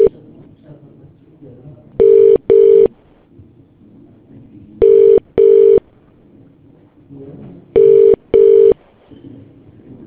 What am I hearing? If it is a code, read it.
Telephone call progress tone (UK), heard as a telephone network's tone, about 400, 450 Hz. Ringback tone.